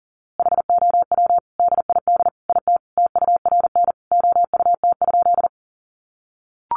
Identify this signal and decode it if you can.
Morse code, heard as a keyed tone, about 40 wpm, 700 Hz. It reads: HOW DID IT TURN OUT?
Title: A Sound Of Thunder 40wpm